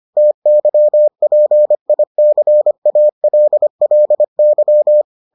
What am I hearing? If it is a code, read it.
Morse code, heard as a keyed tone, about 25 wpm, 600 Hz. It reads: TYPICALLY